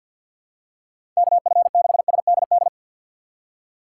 Morse code, heard as a keyed tone, about 46 wpm, 700 Hz. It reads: KF6SDD
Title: USA Call Signs - 3x Character Spacing - 50wpm